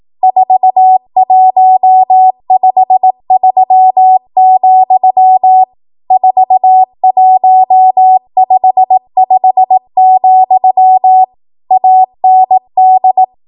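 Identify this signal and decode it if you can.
Morse code, heard as a keyed tone, about 18 wpm, 750 Hz. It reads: 4153, 4155, AND
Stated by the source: ARRL 18 wpm Morse code practice archive transmission.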